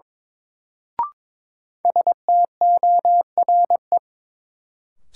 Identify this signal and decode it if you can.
Morse code, heard as a keyed tone, about 21 wpm, 700 Hz. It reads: STORE
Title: Sentences from Top 1000 Words - Review Part 5 - New Word Drill - 22wpm